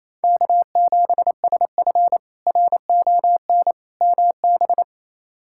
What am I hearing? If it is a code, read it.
Morse code, heard as a keyed tone, about 28 wpm, 700 Hz. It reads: K7SF RON MB
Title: K1USN SST QSO Practice - 28wpm